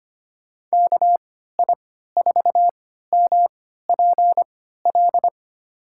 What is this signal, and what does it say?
Morse code, heard as a keyed tone, about 25 wpm, spaced as for 14 wpm, 700 Hz. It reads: KI4MPL
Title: USA Call Signs - 6 Characters - 3x Character Spacing - 25wpm